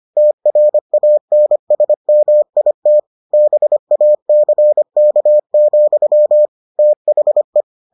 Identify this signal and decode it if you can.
Morse code, heard as a keyed tone, about 25 wpm, 600 Hz. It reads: TRANSMIT BACK, THE